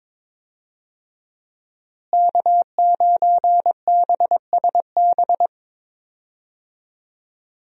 Morse code, heard as a keyed tone, about 21 wpm, 700 Hz. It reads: K9BSB
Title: USA Call Signs - 5 Characters - Repeated 3x - 22wpm